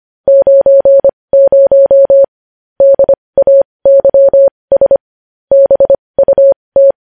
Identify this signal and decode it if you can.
Morse code, heard as a keyed tone, about 25 wpm, spaced as for 20 wpm, 550 Hz. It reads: 90 DAYS BUT
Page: news headlines in morse code at 20 wpm